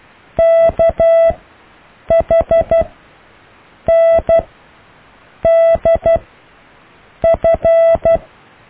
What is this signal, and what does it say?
Morse code, heard as a keyed tone, about 12 wpm, spaced as for 6 wpm, 660 Hz. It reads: KHNDF
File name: cw / EKTPV+BFHNW